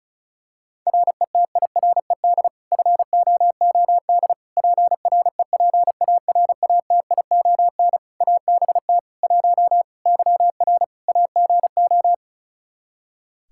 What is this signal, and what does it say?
Morse code, heard as a keyed tone, about 36 wpm, 700 Hz. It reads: RETIRED FOOD PREPARATION ABT 1 YR AGO
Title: Practice QSO - Exchange 3 - More Info - 35wpm